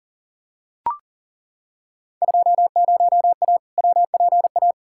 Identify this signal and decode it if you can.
Morse code, heard as a keyed tone, about 40 wpm, 700 Hz. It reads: 20A WPA
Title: Field Day Exchanges Encore - 40wpm